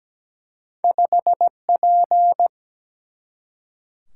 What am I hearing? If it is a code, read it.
Morse code, heard as a keyed tone, about 17 wpm, 700 Hz. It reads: 5P
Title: Single-Letter Single-Number 17wpm